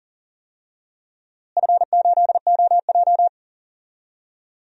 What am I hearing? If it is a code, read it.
Morse code, heard as a keyed tone, about 40 wpm, 700 Hz. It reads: F8OJ
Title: DXCC-100 Call Signs - 4 Characters - Repeated 3x - 40wpm